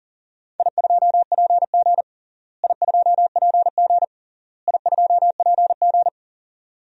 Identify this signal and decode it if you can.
Morse code, heard as a keyed tone, about 40 wpm, 700 Hz. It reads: I2PG I2PG I2PG
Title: DXCC-100 Call Signs - 4 Characters - Repeated 3x - 40wpm